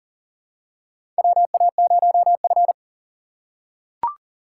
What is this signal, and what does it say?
Morse code, heard as a keyed tone, about 40 wpm, 700 Hz. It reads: WA0F
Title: USA Call Signs - 4 Characters - 2x Character Spacing - 40wpm